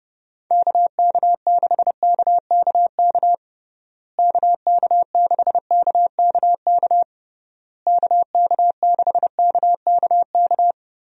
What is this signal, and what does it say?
Morse code, heard as a keyed tone, about 30 wpm, 700 Hz. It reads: KK6KKK KK6KKK KK6KKK
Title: USA Call Signs - 6 Characters - Repeated 3x - 30wpm